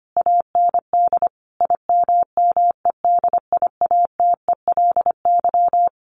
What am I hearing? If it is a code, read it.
Morse code, heard as a keyed tone, about 25 wpm, 700 Hz. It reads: AND IMMEDIATELY
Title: Sentences from Top 1100 Words 25wpm